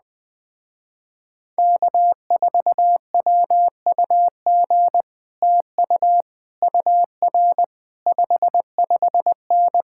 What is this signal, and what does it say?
Morse code, heard as a keyed tone, about 20 wpm, 700 Hz. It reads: K4WUG TU UR 55N
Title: Speed-Racing Common QSO Phrases - 17wpm